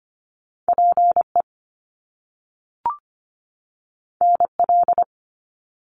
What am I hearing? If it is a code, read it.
Morse code, heard as a keyed tone, about 25 wpm, 700 Hz. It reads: PE NL
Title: Canadian Provinces and Territories 25wpm